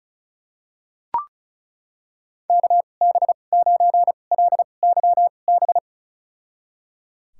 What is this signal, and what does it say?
Morse code, heard as a keyed tone, about 36 wpm, spaced as for 25 wpm, 700 Hz. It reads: KB9LYB
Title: USA Call Signs - 6 Characters - 2x Character Spacing - 35wpm